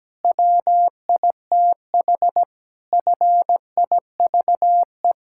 Morse code, heard as a keyed tone, about 17 wpm, 700 Hz. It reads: WITH FIVE
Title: Pangram Challenge v2 - 17wpm